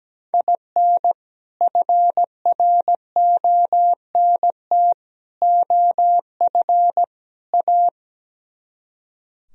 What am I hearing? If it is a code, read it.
Morse code, heard as a keyed tone, about 17 wpm, 700 Hz. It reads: IN FRONT OF A
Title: Speed-Racing - Sets of 4 Words 17wpm